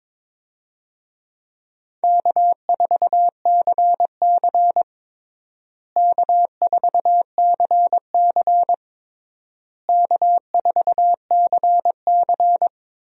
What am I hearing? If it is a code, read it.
Morse code, heard as a keyed tone, about 22 wpm, 700 Hz. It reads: K4CC K4CC K4CC
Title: USA Call Signs - 4 Characters - Repeated 3x - 22wpm